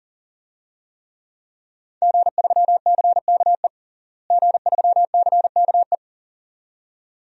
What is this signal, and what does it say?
Morse code, heard as a keyed tone, about 37 wpm, 700 Hz. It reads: G3CKE G3CKE
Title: DXCC-100 Call Signs - 5 Characters - Repeated 2x - 40wpm